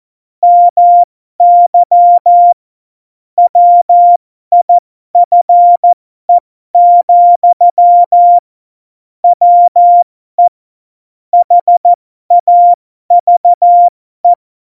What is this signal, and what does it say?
Morse code, heard as a keyed tone, about 14 wpm, spaced as for 12 wpm, 700 Hz. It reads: MY WIFE, WE HAVE
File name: Morsekurs / Witze